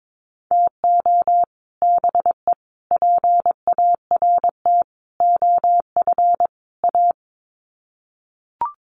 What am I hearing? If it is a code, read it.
Morse code, heard as a keyed tone, about 22 wpm, 700 Hz. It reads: TO BE PART OF A
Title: Sets of 5 Words Encore2 22wpm